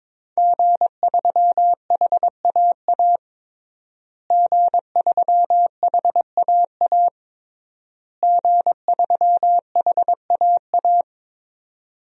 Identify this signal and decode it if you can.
Morse code, heard as a keyed tone, about 22 wpm, 700 Hz. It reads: G3HAA G3HAA G3HAA
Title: DXCC-100 Call Signs - 5 Characters - Repeated 3x - 22wpm